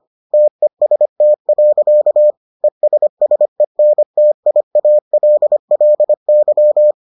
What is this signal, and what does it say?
Morse code, heard as a keyed tone, about 25 wpm, 600 Hz. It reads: TEST. ESSENTIALLY